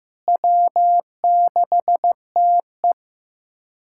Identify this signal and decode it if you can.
Morse code, heard as a keyed tone, about 14 wpm, 700 Hz. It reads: W6TE
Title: USA Callsign DE Callsign - 4 Characters - 15wpm